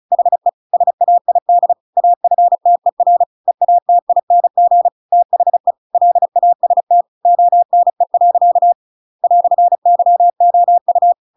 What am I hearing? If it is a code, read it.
Morse code, heard as a keyed tone, about 34 wpm, 700 Hz. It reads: HE SAID AFTER EATING THE LAST ONE. "YOU
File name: Pinocchio / Pinocchio ch7 35 35 ik1hge